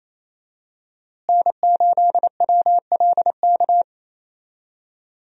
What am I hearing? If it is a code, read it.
Morse code, heard as a keyed tone, about 27 wpm, 700 Hz. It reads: N8WLK
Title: USA Call Signs - 5 Characters - Repeated 3x - 28wpm